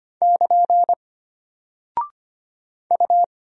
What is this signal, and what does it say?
Morse code, heard as a keyed tone, about 25 wpm, 700 Hz. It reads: <KN> U
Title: Single Character (Letter:Number:Punctuation:Prosign) Alt 25wpm